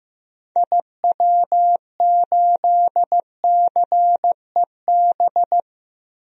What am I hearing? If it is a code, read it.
Morse code, heard as a keyed tone, about 15 wpm, 700 Hz. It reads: IW8CEB
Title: DXCC-100 Call Signs - 6 Characters - Repeated 2x - 15wpm